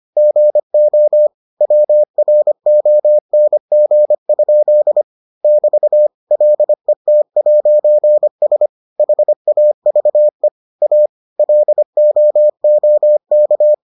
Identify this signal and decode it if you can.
Morse code, heard as a keyed tone, about 25 wpm, 600 Hz. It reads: GO WRONG? <BT> LET'S HAVE A LOOK